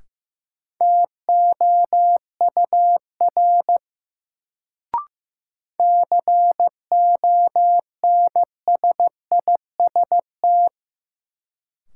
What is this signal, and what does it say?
Morse code, heard as a keyed tone, about 15 wpm, 700 Hz. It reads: TOUR CONSIST
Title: Sentences from Top 1400 Words - New Word Drill Alt - 15wpm